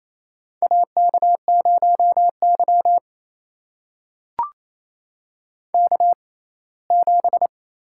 Morse code, heard as a keyed tone, about 28 wpm, spaced as for 19 wpm, 700 Hz. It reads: AK0Y K7
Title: USA Call Signs - 4 Characters - 6x Character Spacing - 28wpm